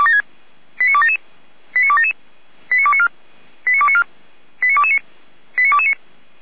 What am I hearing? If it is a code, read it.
Five-tone selective call (ZVEI), heard as quick runs of five tones, tones of about 70 ms. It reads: 89277 89277 89294 89294 89229 89229